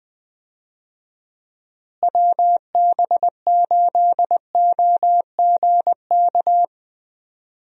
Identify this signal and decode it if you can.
Morse code, heard as a keyed tone, about 20 wpm, 700 Hz. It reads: WB8OGK
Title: USA Call Signs - 6 Characters - Repeated 3x - 20wpm